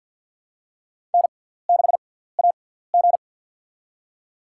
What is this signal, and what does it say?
Morse code, heard as a keyed tone, about 53 wpm, spaced as for 17 wpm, 700 Hz. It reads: N6AG
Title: USA Call Signs - 4 Characters - 6x Character Spacing - 50wpm